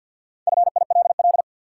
Morse code, heard as a keyed tone, about 53 wpm, 700 Hz. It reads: FILL
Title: Sentences from Top 1500 Words - New Word Drill Alt - 50wpm